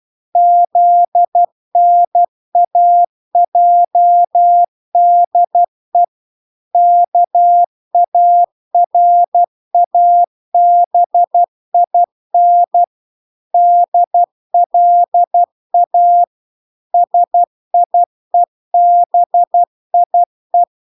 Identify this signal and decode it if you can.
Morse code, heard as a keyed tone, about 12 wpm, 700 Hz. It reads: ZNAJDE KARABIN DLA SIEBIE